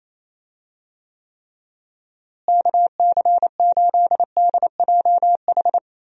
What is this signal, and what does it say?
Morse code, heard as a keyed tone, about 28 wpm, 700 Hz. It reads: KC8DJH
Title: USA Call Signs - 6 Characters - Repeated 3x - 28wpm